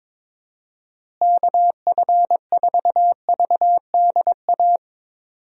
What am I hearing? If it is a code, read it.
Morse code, heard as a keyed tone, about 22 wpm, 700 Hz. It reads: KF4VDA
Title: USA Call Signs - 6 Characters - Repeated 2x - 22wpm